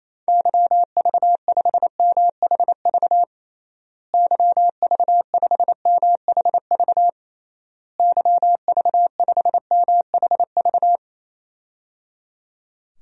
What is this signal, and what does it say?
Morse code, heard as a keyed tone, about 28 wpm, 700 Hz. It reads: YV5MHV YV5MHV YV5MHV
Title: DXCC-100 Call Signs - 6 Characters - Repeated 3x - 28wpm